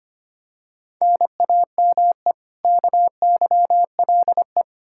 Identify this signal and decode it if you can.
Morse code, heard as a keyed tone, about 25 wpm, 700 Hz. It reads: NAME KYLE
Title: Practice QSO - Exchange 1 - Beginning - 25wpm